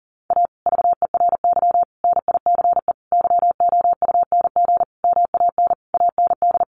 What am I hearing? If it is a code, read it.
Morse code, heard as a keyed tone, about 40 wpm, 700 Hz. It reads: A VERY NICE YOUNG MAN AND
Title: Sentences from Top 1200 Words 40wpm